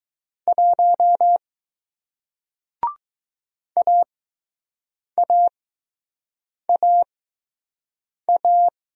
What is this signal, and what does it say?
Morse code, heard as a keyed tone, about 20 wpm, spaced as for 6 wpm, 700 Hz. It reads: 1 AAAA